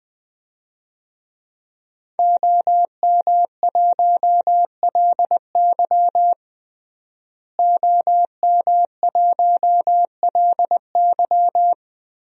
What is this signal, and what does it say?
Morse code, heard as a keyed tone, about 20 wpm, 700 Hz. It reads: OM1LY OM1LY
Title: DXCC-100 Call Signs - 5 Characters - Repeated 3x - 20wpm